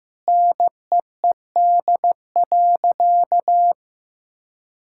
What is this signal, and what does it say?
Morse code, heard as a keyed tone, about 15 wpm, 700 Hz. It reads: NEED.
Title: Sentences from Top 1000 Words - Review Part 5 - 15wpm